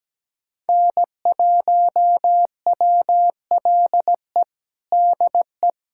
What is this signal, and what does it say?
Morse code, heard as a keyed tone, about 17 wpm, 700 Hz. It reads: N1WLE DE
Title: USA Callsign DE Callsign - 5 Characters - 17wpm